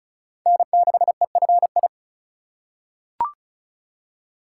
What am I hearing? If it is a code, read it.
Morse code, heard as a keyed tone, about 34 wpm, 700 Hz. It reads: N6EFI
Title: USA Call Signs - 5 Characters - 3x Character Spacing - 35wpm